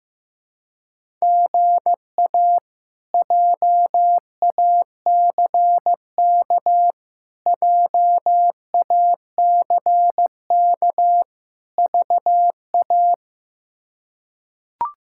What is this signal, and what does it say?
Morse code, heard as a keyed tone, about 15 wpm, 700 Hz. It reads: GA JACK JACK VA
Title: K1USN SST QSO Practice - 15wpm